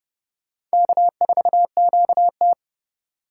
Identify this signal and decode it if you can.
Morse code, heard as a keyed tone, about 38 wpm, 700 Hz. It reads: K4QT